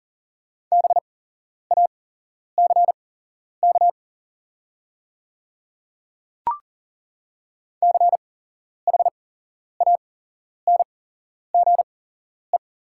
Morse code, heard as a keyed tone, about 40 wpm, spaced as for 11 wpm, 700 Hz. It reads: BACK CHANGE
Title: 8x Character Spacing - Top 300 Words - 40wpm